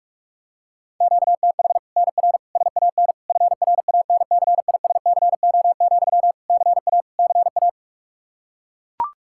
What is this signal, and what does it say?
Morse code, heard as a keyed tone, about 41 wpm, 700 Hz. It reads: QTH NR SAN FRANCISCO, CA CA
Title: Practice QSO - Exchange 1 - Beginning - 45wpm